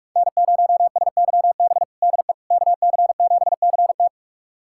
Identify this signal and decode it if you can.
Morse code, heard as a keyed tone, about 46 wpm, 700 Hz. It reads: N0SYB DE KC7CT
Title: USA Callsign DE Callsign - 5 Characters - 45wpm